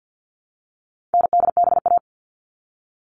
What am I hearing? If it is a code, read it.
Morse code, heard as a keyed tone, about 48 wpm, 700 Hz. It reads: NDBA
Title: All Cut Numbers - 4-Digit 50wpm